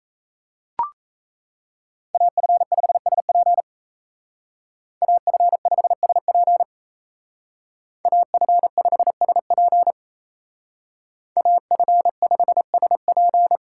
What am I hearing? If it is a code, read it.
Morse code, heard as a keyed tone, about 33 wpm, 700 Hz. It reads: AF5SP AF5SP AF5SP AF5SP
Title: Speed Racing - USA 2x2 Call Signs 28wpm